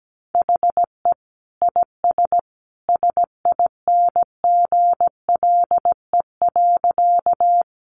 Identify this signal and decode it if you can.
Morse code, heard as a keyed tone, about 17 wpm, 700 Hz. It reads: HE IS SINGLE.
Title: Sentences from Top 700 Words 17wpm